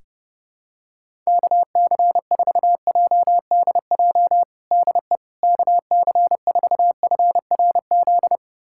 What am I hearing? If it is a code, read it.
Morse code, heard as a keyed tone, about 32 wpm, 700 Hz. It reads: KC4JDJ DE KC4FRZ
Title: USA Callsign DE Callsign - 6 Characters - 30wpm